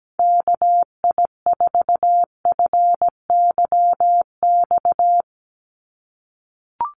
Morse code, heard as a keyed tone, about 17 wpm, 700 Hz. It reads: KI4FYX
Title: USA Call Signs 17wpm